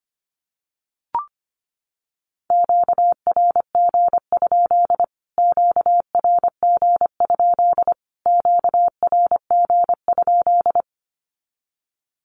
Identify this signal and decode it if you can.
Morse code, heard as a keyed tone, about 25 wpm, 700 Hz. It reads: QRG? QRG? QRG?
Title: Q-Codes Repeated 3x 25wpm